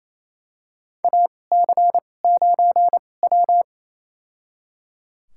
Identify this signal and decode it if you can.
Morse code, heard as a keyed tone, about 28 wpm, spaced as for 20 wpm, 700 Hz. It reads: AC9W
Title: USA Call Signs - 4 Characters - 2x Character Spacing - 28wpm